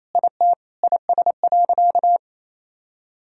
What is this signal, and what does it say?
Morse code, heard as a keyed tone, about 28 wpm, 700 Hz. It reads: IT IS.
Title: Sentences from Top 1000 Words - Review Part 1 - 28wpm